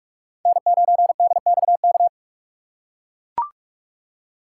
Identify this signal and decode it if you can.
Morse code, heard as a keyed tone, about 41 wpm, 700 Hz. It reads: N9DXK
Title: USA Call Signs - 5 Characters - 2x Character Spacing - 45wpm